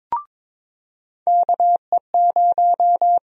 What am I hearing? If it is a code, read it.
Morse code, heard as a keyed tone, about 22 wpm, 700 Hz. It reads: KE0
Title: USA Call Sign Prefixes - 22wpm